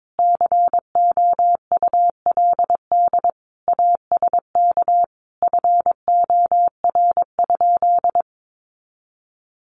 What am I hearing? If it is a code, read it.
Morse code, heard as a keyed tone, about 22 wpm, 700 Hz. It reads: COULD ASK FOR?
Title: Sentences from Top 1400 Words 22wpm